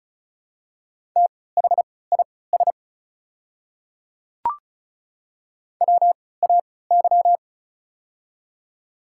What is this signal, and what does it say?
Morse code, heard as a keyed tone, about 36 wpm, spaced as for 20 wpm, 700 Hz. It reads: THIS WAY
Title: 3x Character Spacing - Top 100 Words - 35wpm